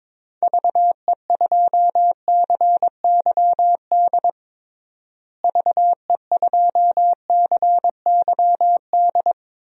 Morse code, heard as a keyed tone, about 22 wpm, 700 Hz. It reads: VE2CYD VE2CYD
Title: DXCC-100 Call Signs - 6 Characters - Repeated 2x - 22wpm